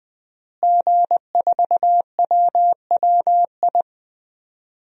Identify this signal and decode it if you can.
Morse code, heard as a keyed tone, about 20 wpm, 700 Hz. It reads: G4WWI
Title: DXCC-100 Call Signs - 5 Characters - Repeated 3x - 20wpm